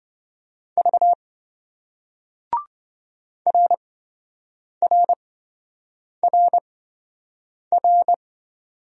Morse code, heard as a keyed tone, about 27 wpm, spaced as for 7 wpm, 700 Hz. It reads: V RRRR